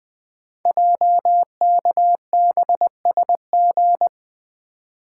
Morse code, heard as a keyed tone, about 20 wpm, 700 Hz. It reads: JKBSG
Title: Code Groups - 5 Letters - 20wpm